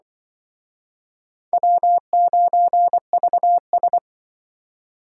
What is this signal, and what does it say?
Morse code, heard as a keyed tone, about 24 wpm, 700 Hz. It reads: W9VS
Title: Speed Racing - USA 1x2 Call Signs 20wpm